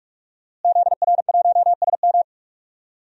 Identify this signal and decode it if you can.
Morse code, heard as a keyed tone, about 44 wpm, 700 Hz. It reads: ZR1SM
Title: DXCC-100 Call Signs - 5 Characters - Repeated 3x - 45wpm